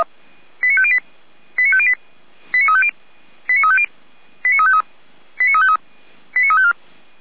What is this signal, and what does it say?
Five-tone selective call (ZVEI), heard as quick runs of five tones, tones of about 70 ms. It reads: 89508 89508 89350 89350 89353 89353 89354